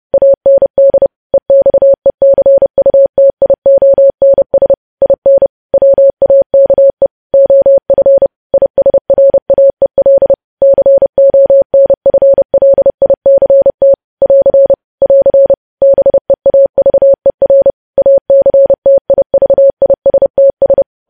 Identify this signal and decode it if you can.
Morse code, heard as a keyed tone, about 30 wpm, 550 Hz. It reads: AND EXECUTIONS IN WAKE OF ISRAEL CONFLICT <AR> <AR> BEAVER ACTIVISTS
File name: podcast / News headlines-1x-30x30